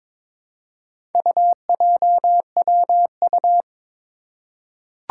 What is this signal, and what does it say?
Morse code, heard as a keyed tone, about 22 wpm, 700 Hz. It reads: UJWU
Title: Four Challenging Letter-Number Combinations 22wpm